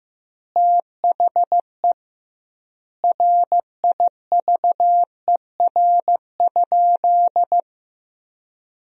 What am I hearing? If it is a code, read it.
Morse code, heard as a keyed tone, about 15 wpm, 700 Hz. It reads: THE RIVER?
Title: 2x Word-Spacing - Sentences from Top 300 Words - 15wpm